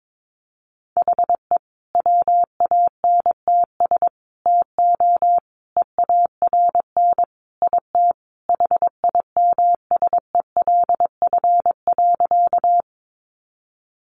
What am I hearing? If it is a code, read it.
Morse code, heard as a keyed tone, about 22 wpm, 700 Hz. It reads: HE WANTS TO EARN IT HIMSELF.
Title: Additional Easy Phrases 22wpm